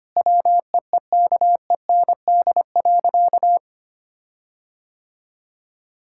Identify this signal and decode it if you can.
Morse code, heard as a keyed tone, about 25 wpm, 700 Hz. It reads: WEEKEND.
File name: practice / 25wpm / Difficult Sentences with Hints - 25wpm